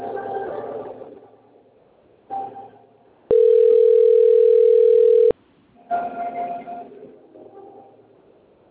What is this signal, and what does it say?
Ringback tone (North America), 440 + 480 Hz.